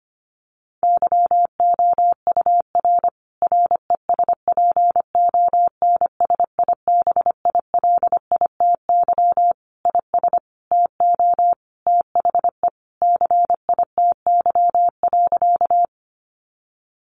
Morse code, heard as a keyed tone, about 25 wpm, 700 Hz. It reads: YOUR RESPONSIBILITY IS TO THE CITY.
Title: Sentences from Top 1200 Words 25wpm